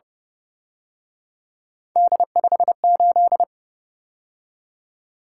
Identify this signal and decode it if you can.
Morse code, heard as a keyed tone, about 31 wpm, 700 Hz. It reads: D58
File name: Speed-Racing - Three Letter-Number Combinations - 25wpm